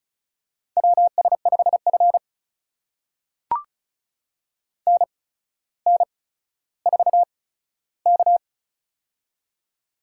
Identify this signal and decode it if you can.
Morse code, heard as a keyed tone, about 36 wpm, spaced as for 15 wpm, 700 Hz. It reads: WS5F NN4K
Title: USA Call Signs - 4 Characters - 8x Character Spacing - 35wpm